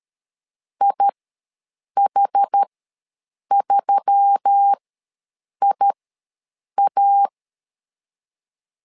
Morse code, heard as a keyed tone, about 13 wpm, spaced as for 7 wpm, 800 Hz. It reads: IH3IA
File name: LEZIONI / 4 Q1S3 ws 12wpm 2038